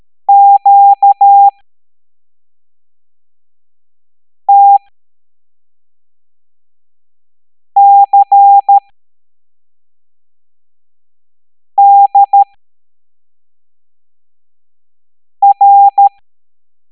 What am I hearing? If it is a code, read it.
Morse code, heard as a keyed tone, about 13 wpm, spaced as for 3 wpm, 800 Hz. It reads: QTCDR
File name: hamdata / L07-EX02-HQP